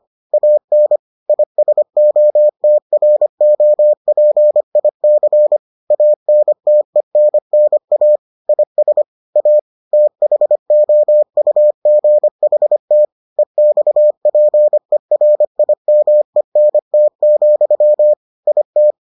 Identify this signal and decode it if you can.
Morse code, heard as a keyed tone, about 25 wpm, 600 Hz. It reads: AN ISOTROPIC ANTENNA IS A THOUGHT EXPERIMENT, IT